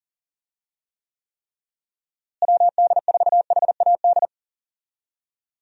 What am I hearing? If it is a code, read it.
Morse code, heard as a keyed tone, about 40 wpm, 700 Hz. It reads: WD4HAD
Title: USA Call Signs - 6 Characters - Repeated 3x - 40wpm